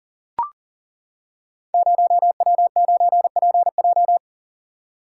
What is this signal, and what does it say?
Morse code, heard as a keyed tone, about 42 wpm, 700 Hz. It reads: 0W9PJ
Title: Code Groups - 5 Letter:Numbers - 40wpm